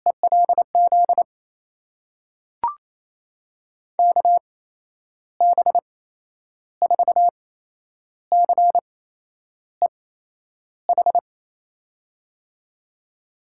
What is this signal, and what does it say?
Morse code, heard as a keyed tone, about 28 wpm, spaced as for 10 wpm, 700 Hz. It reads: ELZ KB4CEH
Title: USA Call Signs - 8x Character Spacing - 28wpm